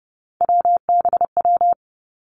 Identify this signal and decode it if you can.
Morse code, heard as a keyed tone, about 30 wpm, 700 Hz. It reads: WBW
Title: Three Challenging Letter-Number Combinations 30wpm